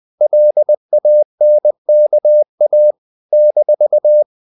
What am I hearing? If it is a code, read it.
Morse code, heard as a keyed tone, about 20 wpm, 600 Hz. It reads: LANKA -
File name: CW / Practice / NationalCapitals20